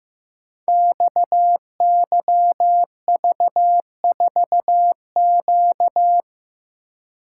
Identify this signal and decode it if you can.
Morse code, heard as a keyed tone, about 15 wpm, 700 Hz. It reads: XYV4Q
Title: Code Groups - 5 Letter:Numbers - 15wpm